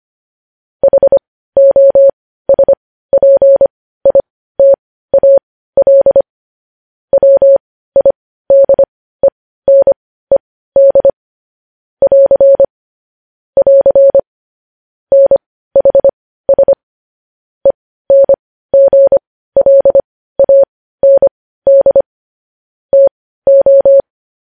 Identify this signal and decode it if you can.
Morse code, heard as a keyed tone, about 25 wpm, spaced as for 15 wpm, 550 Hz. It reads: HOSPITAL WIDENED <AR> <AR> NHS ENGLAND TO
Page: news-headlines-in-morse-code-at-15-wpm